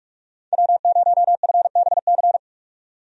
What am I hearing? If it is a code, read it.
Morse code, heard as a keyed tone, about 46 wpm, 700 Hz. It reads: W0FBC
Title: USA Call Signs - 5 Characters - 45wpm